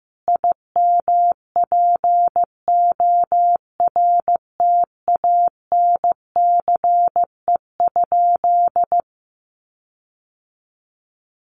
Morse code, heard as a keyed tone, about 15 wpm, 700 Hz. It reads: IMPORTANCE?
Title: Sentences from Top 1500 Words 15wpm